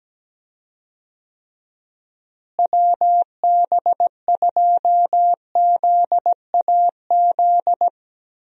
Morse code, heard as a keyed tone, about 17 wpm, 700 Hz. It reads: WB2ZAZ
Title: USA Call Signs - 6 Characters - Repeated 3x - 17wpm